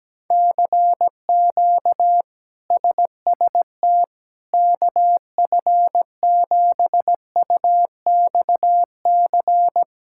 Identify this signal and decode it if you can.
Morse code, heard as a keyed tone, about 17 wpm, 700 Hz. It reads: CQ SST KF7UXC
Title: K1USN SST QSO Practice - 17wpm